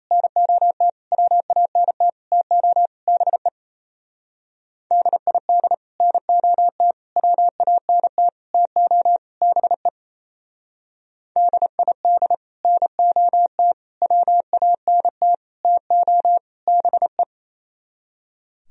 Morse code, heard as a keyed tone, about 33 wpm, 700 Hz. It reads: NOT WANT TO BE DID NOT WANT TO BE DID NOT WANT TO BE
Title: Speed-Racing - Sets of 5 Words 28wpm